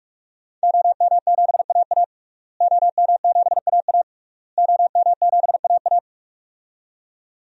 Morse code, heard as a keyed tone, about 46 wpm, 700 Hz. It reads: OM7AA OM7AA OM7AA
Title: DXCC-100 Call Signs - 5 Characters - Repeated 3x - 45wpm